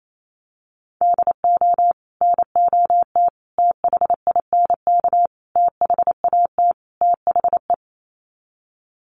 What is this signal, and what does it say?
Morse code, heard as a keyed tone, about 28 wpm, 700 Hz. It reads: DO NOT THINK THAT THE
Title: Sets of 5 Words Encore2 28wpm